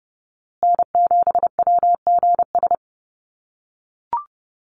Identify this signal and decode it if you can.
Morse code, heard as a keyed tone, about 30 wpm, 700 Hz. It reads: N7WGS
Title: USA Call Signs Repeated 3x 30wpm